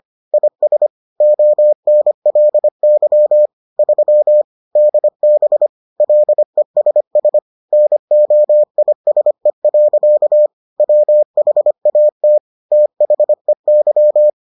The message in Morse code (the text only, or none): IS ONLY 3 DB LESS NOISE. WHAT THEY